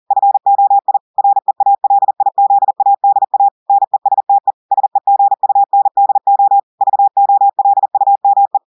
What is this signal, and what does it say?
Morse code, heard as a keyed tone, about 40 wpm, 800 Hz. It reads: FOI REALIZADA NESTE SEGUNDO VOLUME
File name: audios / portugais / portugais texte en clair 1 vitesse 40 mots minutes